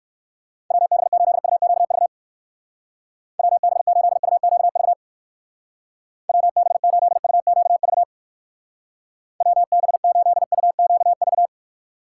Morse code, heard as a keyed tone, about 59 wpm, 700 Hz. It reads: WB8UQV WB8UQV WB8UQV WB8UQV